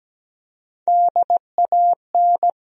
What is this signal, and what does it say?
Morse code, heard as a keyed tone, about 17 wpm, 700 Hz. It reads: DAN
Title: USA Common Names - Max 4 Letters - 17wpm